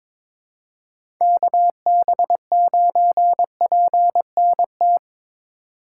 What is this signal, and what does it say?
Morse code, heard as a keyed tone, about 22 wpm, 700 Hz. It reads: KB9PNT
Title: USA Call Signs - 6 Characters - 4x Character Spacing - 22wpm